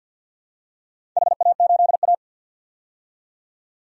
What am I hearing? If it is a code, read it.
Morse code, heard as a keyed tone, about 53 wpm, 700 Hz. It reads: HA8A
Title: DXCC-100 Call Signs - 4 Characters - 50wpm